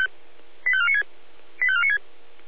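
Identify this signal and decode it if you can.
Five-tone selective call (ZVEI), heard as quick runs of five tones, tones of about 70 ms. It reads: 85496 85496
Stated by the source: Five-tone ZVEI selective call received on SDR.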